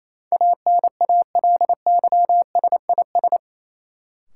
Morse code, heard as a keyed tone, about 28 wpm, 700 Hz. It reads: ANALYSIS
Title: Sentences from Top 1000 Words - Review Part 5 - New Word Drill - 28wpm